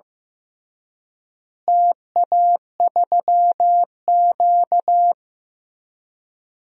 Morse code, heard as a keyed tone, about 15 wpm, 700 Hz. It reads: TA3Q
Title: DXCC-100 Call Signs - 4 Characters - Repeated 3x - 15wpm